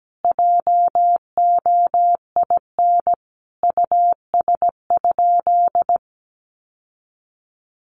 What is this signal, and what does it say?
Morse code, heard as a keyed tone, about 17 wpm, 700 Hz. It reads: JOIN US?
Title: Sentences from Top 1400 Words 17wpm